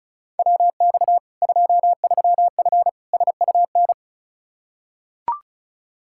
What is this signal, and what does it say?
Morse code, heard as a keyed tone, about 34 wpm, 700 Hz. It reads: WX 23F SUN
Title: Practice QSO - Exchange 2 - Info - 35wpm